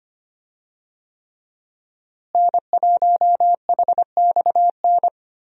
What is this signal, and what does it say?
Morse code, heard as a keyed tone, about 25 wpm, 700 Hz. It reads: N1HXN
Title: USA Call Signs - 5 Characters - Repeated 3x - 25wpm